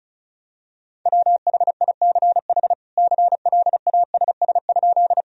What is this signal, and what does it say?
Morse code, heard as a keyed tone, about 36 wpm, 700 Hz. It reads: WHICH CLASS?
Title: Sentences from Top 1000 Words - Review Part 2 - 35wpm